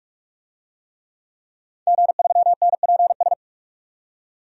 Morse code, heard as a keyed tone, about 44 wpm, 700 Hz. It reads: G3NPS
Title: DXCC-100 Call Signs - 5 Characters - 45wpm